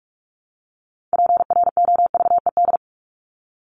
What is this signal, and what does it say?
Morse code, heard as a keyed tone, about 44 wpm, 700 Hz. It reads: PROVED
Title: ED Suffix - Top 100 Words with Base Words 45wpm